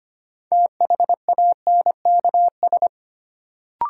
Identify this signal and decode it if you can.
Morse code, heard as a keyed tone, about 25 wpm, 700 Hz. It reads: THANKS
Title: Sets of 5 Words - New Word Drill - 25wpm